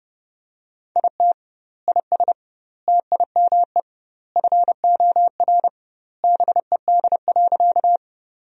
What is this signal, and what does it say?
Morse code, heard as a keyed tone, about 30 wpm, 700 Hz. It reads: IT IS TIME FOR BED.
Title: 2x Word-Spacing - Sentences from Top 500 Words - 30wpm